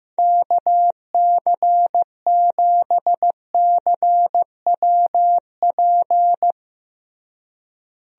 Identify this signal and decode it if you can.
Morse code, heard as a keyed tone, about 15 wpm, 700 Hz. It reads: KC7CWP
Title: USA Call Signs - 6 Characters - Repeated 3x - 15wpm